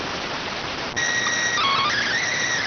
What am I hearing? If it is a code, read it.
SSTV (slow-scan television), PD 290.